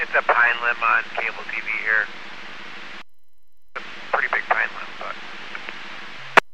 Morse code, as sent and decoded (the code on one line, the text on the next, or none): none